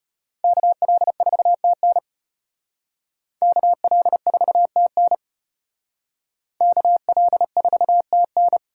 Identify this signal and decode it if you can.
Morse code, heard as a keyed tone, about 32 wpm, 700 Hz. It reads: KL4TN KL4TN KL4TN